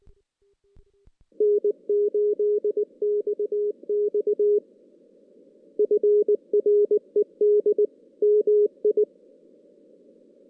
Morse code, heard as a keyed tone, about 20 wpm, 420 Hz. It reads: N8XX FRED MI
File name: 2016 JAN CW NAQP / 2016 Jan naqp k5ot - 0-6